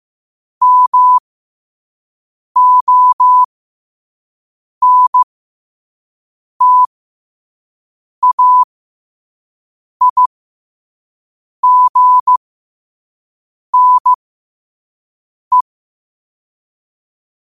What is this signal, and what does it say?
Morse code, heard as a keyed tone, about 15 wpm, spaced as for 5 wpm, 1000 Hz. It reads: MONTAIGNE
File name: qotdfiles / 5wpm